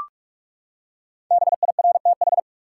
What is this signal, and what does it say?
Morse code, heard as a keyed tone, about 41 wpm, 700 Hz. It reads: BIRTH
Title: Five Letter Words - 45wpm